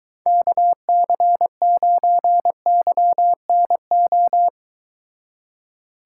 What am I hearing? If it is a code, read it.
Morse code, heard as a keyed tone, about 23 wpm, 700 Hz. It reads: KC9YNO